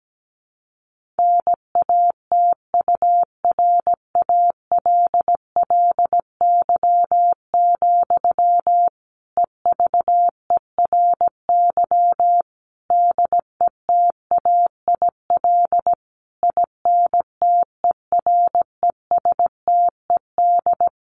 Morse code, heard as a keyed tone, about 17 wpm, 700 Hz. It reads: NATURALLY, EVERY DETAIL INTERESTED
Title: Sentences from Top 1500 Words 17wpm